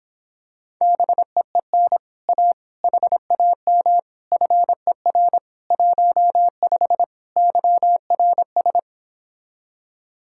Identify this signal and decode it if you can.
Morse code, heard as a keyed tone, about 26 wpm, 700 Hz. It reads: BEEN A HAM FER 15 YRS